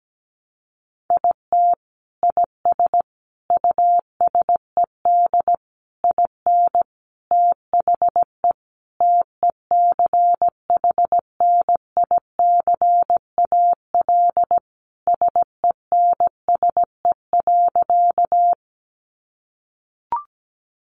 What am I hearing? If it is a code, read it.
Morse code, heard as a keyed tone, about 17 wpm, 700 Hz. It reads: IT IS USED IN THE TECHNICAL SENSE.
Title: Sentences from Top 1100 Words 17wpm